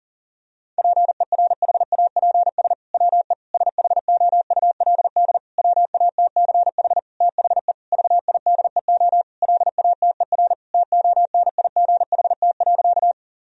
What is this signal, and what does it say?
Morse code, heard as a keyed tone, about 40 wpm, 700 Hz. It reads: PERHAPS WE SHOULD WATCH THE VIDEO LATER TONIGHT.
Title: Sentences from Top 1000 Words - Review Part 4 - 40wpm